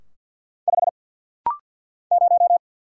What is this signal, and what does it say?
Morse code, heard as a keyed tone, about 49 wpm, 700 Hz. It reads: 5 0
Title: Single Letter-Number - Rapid-Fire 50wpm